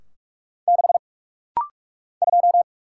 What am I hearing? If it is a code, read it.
Morse code, heard as a keyed tone, about 45 wpm, 700 Hz. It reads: 6 2
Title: Single Letter-Number - Rapid-Fire 45wpm